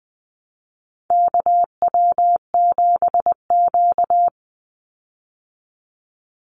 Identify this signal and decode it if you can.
Morse code, heard as a keyed tone, about 20 wpm, 700 Hz. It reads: KW7Q
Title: Call Signs Repeated 3x - 20wpm